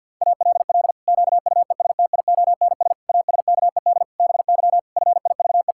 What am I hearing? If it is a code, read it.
Morse code, heard as a keyed tone, about 52 wpm, 700 Hz. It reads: ALL QUESTIONS ASKED BY FIVE
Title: Pangram Challenge v2 - 50wpm